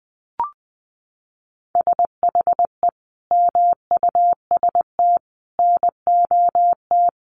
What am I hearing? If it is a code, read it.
Morse code, heard as a keyed tone, about 20 wpm, 700 Hz. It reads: SHE MUST NOT
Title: Sentences from Top 1500 Words 20wpm